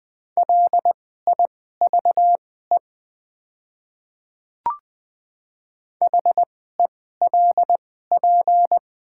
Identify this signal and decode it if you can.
Morse code, heard as a keyed tone, about 20 wpm, spaced as for 14 wpm, 700 Hz. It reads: LIVE HELP
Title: 2x Character Spacing - Top 200 Words - 20wpm